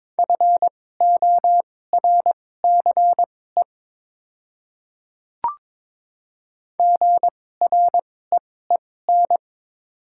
Morse code, heard as a keyed tone, about 22 wpm, spaced as for 16 wpm, 700 Hz. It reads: FORCE GREEN
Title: 2x Character Spacing - Top 500 Words - 22wpm